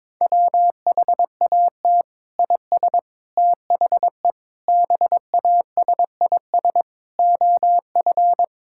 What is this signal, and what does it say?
Morse code, heard as a keyed tone, about 21 wpm, 700 Hz. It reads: WHAT IS THE BASIS OF
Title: Sentences from Top 1000 Words - Review Part 5 - 22wpm